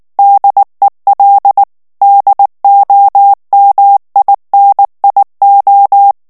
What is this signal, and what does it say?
Morse code, heard as a keyed tone, about 19 wpm, 800 Hz. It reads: DEL DOMINIO
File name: morse mp3 / ESPAGNOL TEXTE EN CLAIR 2 VITESSE 18 MOTS MINUTES